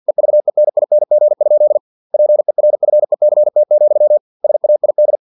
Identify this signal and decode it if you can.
Morse code, heard as a keyed tone, about 50 wpm, 600 Hz. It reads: EVENING? PERFECT, SAID
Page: Never 49wpm